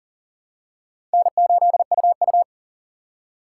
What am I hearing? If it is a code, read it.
Morse code, heard as a keyed tone, about 40 wpm, 700 Hz. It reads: N8UU
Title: USA Call Signs - 4 Characters - 6x Character Spacing - 40wpm